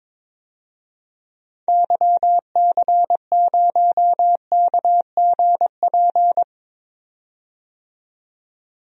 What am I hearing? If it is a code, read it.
Morse code, heard as a keyed tone, about 22 wpm, 700 Hz. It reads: YC0KGP
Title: DXCC-100 Call Signs - 6 Characters - 22wpm